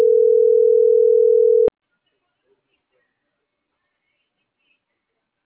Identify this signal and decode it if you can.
Ringback tone (North America), 440 + 480 Hz.